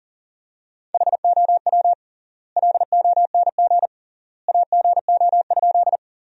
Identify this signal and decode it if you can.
Morse code, heard as a keyed tone, about 40 wpm, 700 Hz. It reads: HOW LONG AGO?
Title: 3x Word-Spacing - Sentences from Top 500 Words - 40wpm